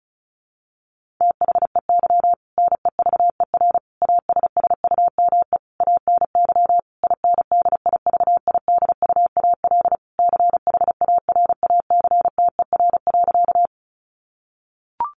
Morse code, heard as a keyed tone, about 34 wpm, 700 Hz. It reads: THEY NEVER ASSUME ANY INDIVIDUAL CHARACTER.
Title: Sentences from Top 1500 Words 35wpm